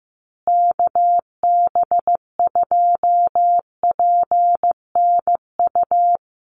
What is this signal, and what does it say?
Morse code, heard as a keyed tone, about 15 wpm, 700 Hz. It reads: KB2PNU
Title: USA Call Signs Repeated 2x 15wpm